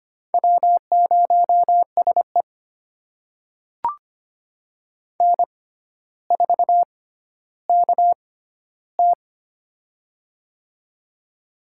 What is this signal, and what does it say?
Morse code, heard as a keyed tone, about 25 wpm, spaced as for 13 wpm, 700 Hz. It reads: W0SE N4KT
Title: USA Call Signs - 4 Characters - 6x Character Spacing - 25wpm